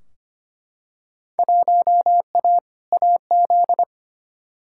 Morse code, heard as a keyed tone, about 25 wpm, 700 Hz. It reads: 1A AZ
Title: Field Day Exchanges - 25wpm